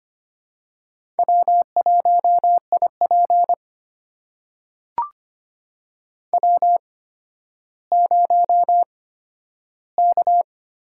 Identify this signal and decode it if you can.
Morse code, heard as a keyed tone, about 25 wpm, spaced as for 12 wpm, 700 Hz. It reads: W1IP W0K
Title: USA Call Signs - 4 Characters - 8x Character Spacing - 25wpm